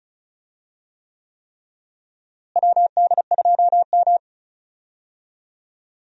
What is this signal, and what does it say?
Morse code, heard as a keyed tone, about 36 wpm, 700 Hz. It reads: WD2M
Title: USA Call Signs - 4 Characters - Repeated 3x - 35wpm